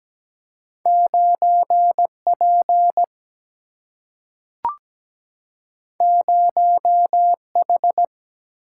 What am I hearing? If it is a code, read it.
Morse code, heard as a keyed tone, about 17 wpm, 700 Hz. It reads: 9P 0H
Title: Single-Letter Single-Number 17wpm